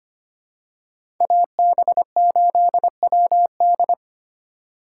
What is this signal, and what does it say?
Morse code, heard as a keyed tone, about 25 wpm, 700 Hz. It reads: AB8WD
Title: USA Call Signs - 5 Characters - 3x Character Spacing - 25wpm